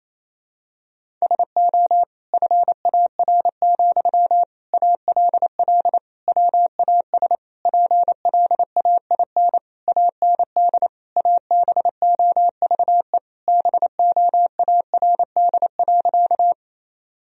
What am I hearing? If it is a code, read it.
Morse code, heard as a keyed tone, about 27 wpm, 700 Hz. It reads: SO FAR, ALL WAS PLAIN AND ABOVE BOARD.
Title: Sentences from Top 1000 Words - Review Part 4 - 28wpm